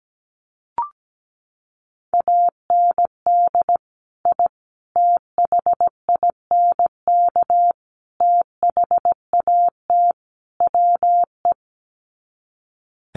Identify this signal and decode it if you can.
Morse code, heard as a keyed tone, about 17 wpm, 700 Hz. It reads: AND I THINK THAT WE
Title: Sets of 5 Words Encore2 17wpm